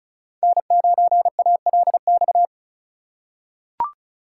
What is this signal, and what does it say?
Morse code, heard as a keyed tone, about 36 wpm, 700 Hz. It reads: N9ALX
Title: USA Call Signs - 5 Characters - 4x Character Spacing - 35wpm